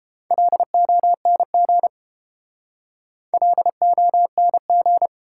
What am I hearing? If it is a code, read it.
Morse code, heard as a keyed tone, about 28 wpm, 700 Hz. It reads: LONG LONG